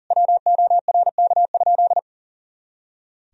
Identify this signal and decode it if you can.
Morse code, heard as a keyed tone, about 40 wpm, 700 Hz. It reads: WORK?
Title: Sentences from Top 500 Words and Abbreviations 40wpm